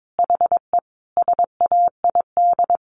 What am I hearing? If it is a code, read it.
Morse code, heard as a keyed tone, about 22 wpm, 700 Hz. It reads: HE SAID
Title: Sentences from Top 1500 Words 22wpm